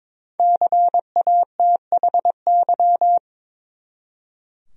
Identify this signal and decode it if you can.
Morse code, heard as a keyed tone, about 22 wpm, 700 Hz. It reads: CATHY
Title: USA Common Names - Max 5 Letters - 22wpm